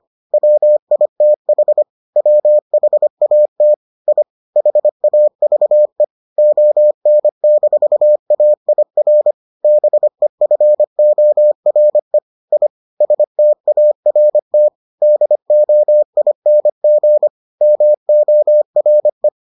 Morse code, heard as a keyed tone, about 25 wpm, 600 Hz. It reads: WITH WHAT I HAVE ON-AIR BEFORE I START DOING MORE